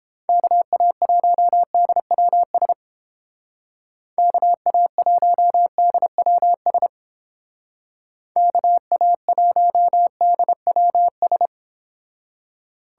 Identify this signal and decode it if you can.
Morse code, heard as a keyed tone, about 31 wpm, 700 Hz. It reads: KA1DWS KA1DWS KA1DWS